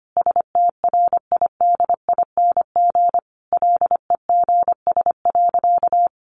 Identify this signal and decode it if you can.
Morse code, heard as a keyed tone, about 25 wpm, 700 Hz. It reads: STRIDING LEGS.
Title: A Sound Of Thunder 25wpm